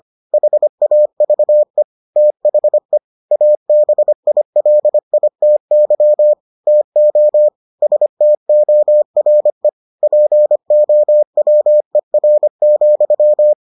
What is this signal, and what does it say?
Morse code, heard as a keyed tone, about 25 wpm, 600 Hz. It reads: HAVE THE ABILITY TO STORE POWER,